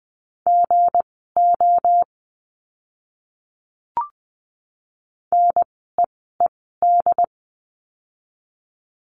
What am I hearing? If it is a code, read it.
Morse code, heard as a keyed tone, about 20 wpm, spaced as for 14 wpm, 700 Hz. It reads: GO NEED
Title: 244 Lesson 39f - Letter z - Words 20wpm